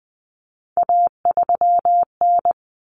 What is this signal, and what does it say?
Morse code, heard as a keyed tone, about 20 wpm, 700 Hz. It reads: A3N
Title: Common Cut Numbers - 3-Digit 20wpm